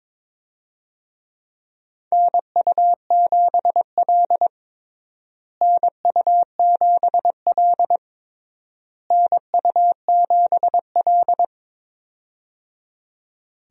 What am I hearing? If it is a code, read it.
Morse code, heard as a keyed tone, about 22 wpm, 700 Hz. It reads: NU7L NU7L NU7L
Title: USA Call Signs - 4 Characters - Repeated 3x - 22wpm